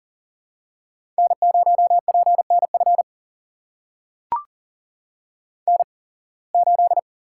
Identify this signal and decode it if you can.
Morse code, heard as a keyed tone, about 40 wpm, spaced as for 26 wpm, 700 Hz. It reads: N0PNF N8
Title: USA Call Signs - 5 Characters - 8x Character Spacing - 40wpm